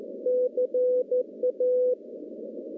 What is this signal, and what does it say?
Morse code, heard as a keyed tone, about 14 wpm, 500 Hz. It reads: CA
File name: contests / CW-SS-k5tr- -0700